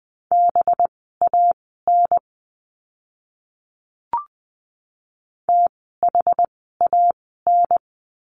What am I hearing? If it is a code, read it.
Morse code, heard as a keyed tone, about 20 wpm, spaced as for 15 wpm, 700 Hz. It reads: BAN THAN
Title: 185 Lesson 30f - Letter b - Words 20wpm